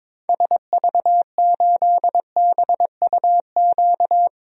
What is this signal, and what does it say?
Morse code, heard as a keyed tone, about 21 wpm, 700 Hz. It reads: SV8BUQ
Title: DXCC-100 Call Signs - 6 Characters - 22wpm